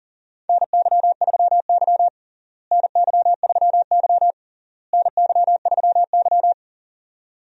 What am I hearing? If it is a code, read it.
Morse code, heard as a keyed tone, about 40 wpm, 700 Hz. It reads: NY3Y NY3Y NY3Y
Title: USA Call Signs - 4 Characters - Repeated 3x - 40wpm